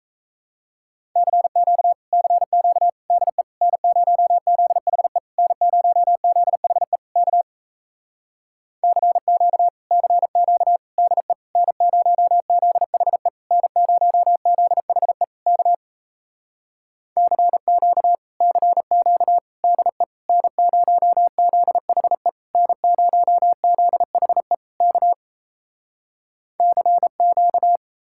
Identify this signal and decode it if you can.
Morse code, heard as a keyed tone, about 37 wpm, 700 Hz. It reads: CQ CQ DE N0ZHE N0ZHE K CQ CQ DE N0ZHE N0ZHE K CQ CQ DE N0ZHE N0ZHE K CQ